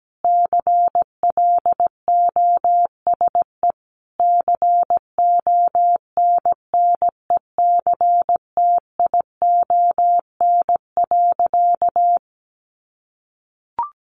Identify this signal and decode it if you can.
Morse code, heard as a keyed tone, about 17 wpm, 700 Hz. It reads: CLOSE CONNECTION.
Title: Sentences from Top 1300 Words 17wpm